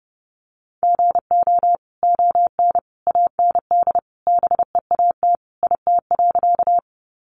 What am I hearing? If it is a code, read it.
Morse code, heard as a keyed tone, about 30 wpm, 700 Hz. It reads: GO ON AND BEAT IT.
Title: Sentences from Top 1300 Words 30wpm